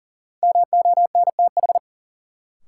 Morse code, heard as a keyed tone, about 40 wpm, 700 Hz. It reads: MONTH
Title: Sets of 5 Words Encore - New Word Drill - 40wpm